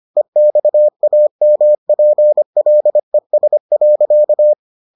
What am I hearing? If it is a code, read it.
Morse code, heard as a keyed tone, about 25 wpm, 600 Hz. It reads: EXAMPLES.